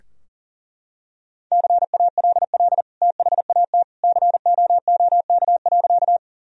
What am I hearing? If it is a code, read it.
Morse code, heard as a keyed tone, about 37 wpm, 700 Hz. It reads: CALL THAT COOK.
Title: Sentences from Top 1000 Words - Review Part 4 - 40wpm